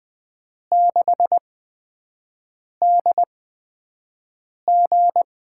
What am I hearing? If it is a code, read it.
Morse code, heard as a keyed tone, about 20 wpm, spaced as for 5 wpm, 700 Hz. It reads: 6DG
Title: USA Call Signs - 8x Character Spacing - 20wpm